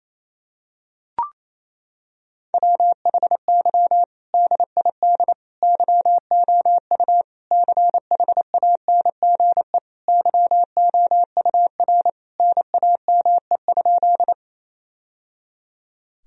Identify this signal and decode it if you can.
Morse code, heard as a keyed tone, about 28 wpm, 700 Hz. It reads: WHY DID YOU CHANGE YOUR NAME?
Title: Sentences from Top 1000 Words - Review Part 1 - 28wpm